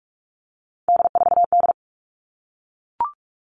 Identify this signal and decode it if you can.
Morse code, heard as a keyed tone, about 44 wpm, 700 Hz. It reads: D4D
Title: All Cut Numbers - 3-Digit 45wpm